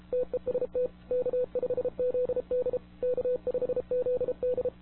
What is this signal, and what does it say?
Morse code, heard as a keyed tone, about 32 wpm, 500 Hz. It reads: TEST K5ZD K5ZD
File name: wwcw08 / k5zd wwcw08 1700z